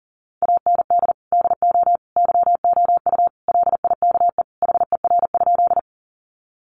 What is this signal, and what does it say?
Morse code, heard as a keyed tone, about 40 wpm, 700 Hz. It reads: AND DO YOU LIKE HER?
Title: Sentences from Top 100 Words 40wpm